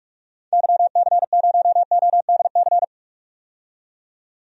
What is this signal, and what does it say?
Morse code, heard as a keyed tone, about 46 wpm, 700 Hz. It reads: YC0ODC
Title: DXCC-100 Call Signs - 6 Characters - 45wpm